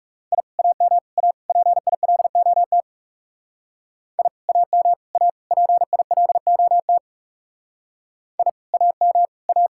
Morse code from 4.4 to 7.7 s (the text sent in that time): AM A PILOT